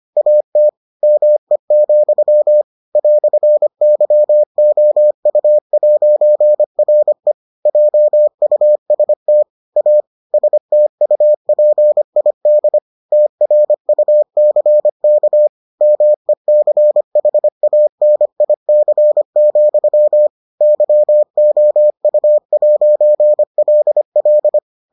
Morse code, heard as a keyed tone, about 25 wpm, 600 Hz. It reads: AT ME, "YOU'RE JUST A STUPID TRUCK MECHANIC, YOU'LL